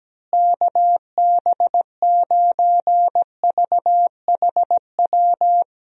Morse code, heard as a keyed tone, about 17 wpm, 700 Hz. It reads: KB9VHW
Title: USA Call Signs - 6 Characters - Repeated 2x - 17wpm